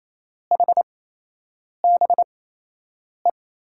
Morse code, heard as a keyed tone, about 28 wpm, spaced as for 8 wpm, 700 Hz. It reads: HBE
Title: USA Call Signs - 6 Characters - 8x Character Spacing - 28wpm